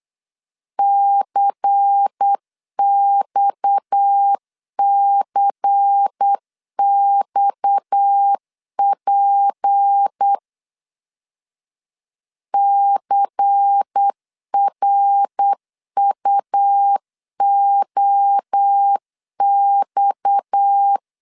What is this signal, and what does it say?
Morse code, heard as a keyed tone, about 8 wpm, 800 Hz. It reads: CXCXP CRUOX
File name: LEZIONI / 6 PKX0 / 6 PKX 8wpm 3041